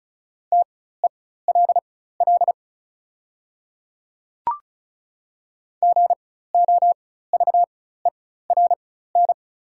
Morse code, heard as a keyed tone, about 36 wpm, spaced as for 16 wpm, 690 Hz. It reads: TELL GOVERN
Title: 4x Character Spacing - Top 500 Words - 35wpm